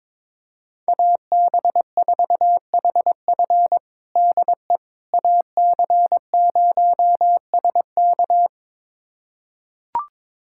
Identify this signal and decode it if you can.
Morse code, heard as a keyed tone, about 21 wpm, 700 Hz. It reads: AB4HF DE AC0SK
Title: USA Callsign DE Callsign - 5 Characters - 22wpm